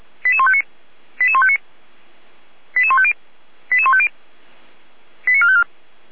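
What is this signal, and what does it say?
Five-tone selective call (ZVEI), heard as quick runs of five tones, tones of about 70 ms. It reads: 88159 88159 88150 88150 89454
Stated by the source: ZVEI five-tone selective call received by SDR.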